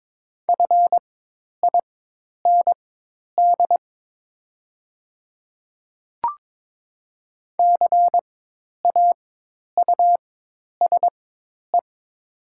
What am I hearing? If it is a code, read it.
Morse code, heard as a keyed tone, about 22 wpm, spaced as for 10 wpm, 700 Hz. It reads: FIND CAUSE
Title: 4x Character Spacing - Top 200 Words - 22wpm